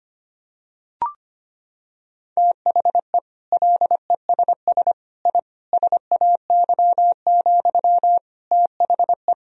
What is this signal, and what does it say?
Morse code, heard as a keyed tone, about 25 wpm, 700 Hz. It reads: THE LESS I SAY, THE
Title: Sentences from Top 1000 Words - Review Part 2 - 25wpm